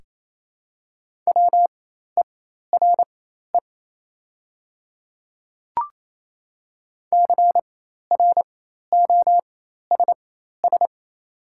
Morse code, heard as a keyed tone, about 28 wpm, spaced as for 13 wpm, 690 Hz. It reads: WERE CROSS
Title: 4x Character Spacing - Top 300 Words - 28wpm